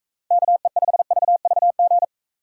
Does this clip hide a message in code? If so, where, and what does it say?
Yes: Morse code from 0.3 to 2.0 s, heard as a keyed tone, about 45 wpm, 700 Hz. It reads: KE5VVG